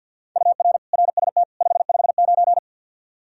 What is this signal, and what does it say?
Morse code, heard as a keyed tone, about 52 wpm, 700 Hz. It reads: UR RST 559
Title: Practice QSO - Exchange 1 - Beginning - 50wpm